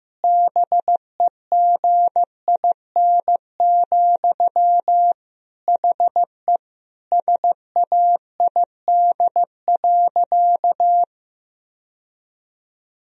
Morse code, heard as a keyed tone, about 15 wpm, 700 Hz. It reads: BEGIN, HE SAID.
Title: Sentences from Top 1000 Words - Review Part 1 - 15wpm